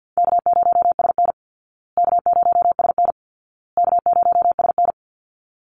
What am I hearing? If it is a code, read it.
Morse code, heard as a keyed tone, about 50 wpm, 700 Hz. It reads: K0SN K0SN K0SN
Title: Call Signs Repeated 2x - 50wpm